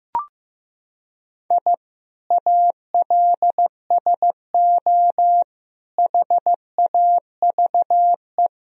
Morse code, heard as a keyed tone, about 16 wpm, 700 Hz. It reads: I ALSO HAVE
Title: Sentences from Top 1000 Words - Review Part 3 - 15wpm